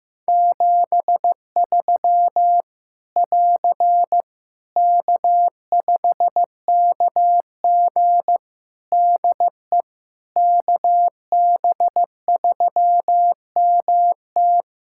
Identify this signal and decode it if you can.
Morse code, heard as a keyed tone, about 15 wpm, 700 Hz. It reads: 73 <AR> K5KG DE KB3MT
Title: K1USN SST QSO Practice 15wpm